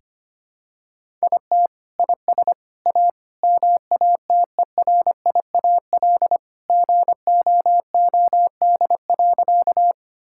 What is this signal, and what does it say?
Morse code, heard as a keyed tone, about 25 wpm, 700 Hz. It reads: IT IS A MATERIAL GOOD.
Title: Sentences from Top 1000 Words - Review Part 5 - 25wpm